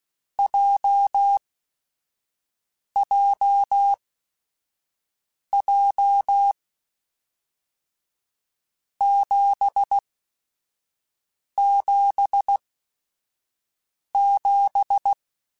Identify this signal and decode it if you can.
Morse code, heard as a keyed tone, about 16 wpm, spaced as for 5 wpm, 800 Hz. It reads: JJJ 777
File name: Groups@5 / Groups@5 mp3s / JZ17@5